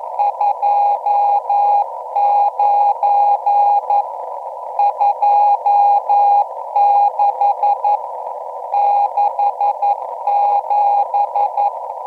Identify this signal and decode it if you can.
Morse code, heard as a keyed tone, about 11 wpm, 820 Hz, with radio noise. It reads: 29 26 67